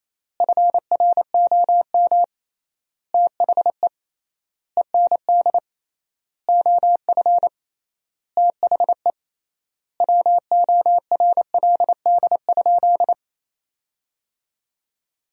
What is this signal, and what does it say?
Morse code, heard as a keyed tone, about 28 wpm, 700 Hz. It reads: FROM THE END OF THE WORLD?
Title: 3x Word-Spacing - Sentences from Top 200 Words - 28wpm